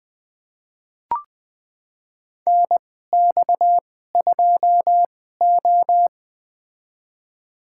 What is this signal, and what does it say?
Morse code, heard as a keyed tone, about 20 wpm, spaced as for 15 wpm, 700 Hz. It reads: NX2O
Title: USA Call Signs - 4 Characters - 2x Character Spacing - 20wpm